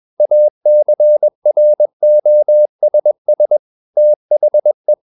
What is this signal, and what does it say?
Morse code, heard as a keyed tone, about 21 wpm, 600 Hz. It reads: ACROSS THE
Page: CW med Gnister